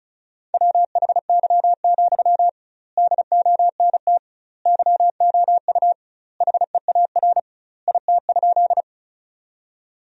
Morse code, heard as a keyed tone, about 34 wpm, 700 Hz. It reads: WHY, DONT YOU HEAR IT?
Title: 2x Word-Spacing - Sentences from Top 300 Words - 35wpm